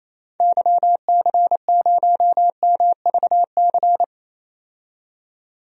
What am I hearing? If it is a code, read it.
Morse code, heard as a keyed tone, about 28 wpm, 700 Hz. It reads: YC0MVC
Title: DXCC-100 Call Signs - 6 Characters - 28wpm